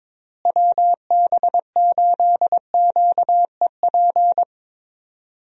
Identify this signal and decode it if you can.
Morse code, heard as a keyed tone, about 21 wpm, 700 Hz. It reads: WB8QEP
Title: USA Call Signs - 6 Characters - Repeated 2x - 22wpm